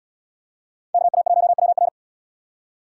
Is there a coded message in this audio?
Yes: Morse code from 0.9 to 1.9 s, heard as a keyed tone, about 69 wpm, 700 Hz. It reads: KA1WR